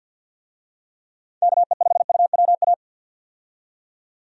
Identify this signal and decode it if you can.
Morse code, heard as a keyed tone, about 51 wpm, 700 Hz. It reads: KE5UWA
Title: USA Call Signs - 6 Characters - 50wpm